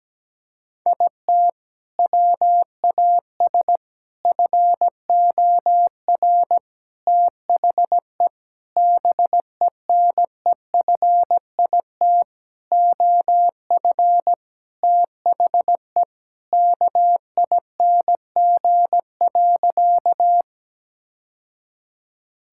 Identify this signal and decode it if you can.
Morse code, heard as a keyed tone, about 17 wpm, 700 Hz. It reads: IT WAS FOR THE BENEFIT OF THE KING.
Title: Sentences from Top 1000 Words - Review Part 6 - 17wpm